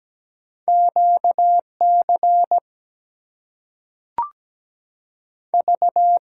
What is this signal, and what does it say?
Morse code, heard as a keyed tone, about 17 wpm, 700 Hz. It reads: QC V
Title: US States and Canadian Provinces 17wpm